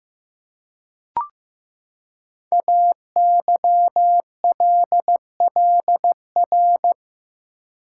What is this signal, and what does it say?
Morse code, heard as a keyed tone, about 15 wpm, 700 Hz. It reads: AYLLR
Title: Code Groups - 5 Letters - 15wpm